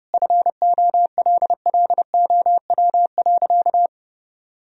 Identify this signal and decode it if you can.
Morse code, heard as a keyed tone, about 30 wpm, 700 Hz. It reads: FOLLOW.
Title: Sentences from Top 1000 Words - Review Part 2 - 30wpm